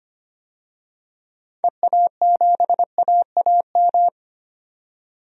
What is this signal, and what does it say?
Morse code, heard as a keyed tone, about 25 wpm, 700 Hz. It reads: EA7AAM
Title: DXCC-100 Call Signs - 6 Characters - 25wpm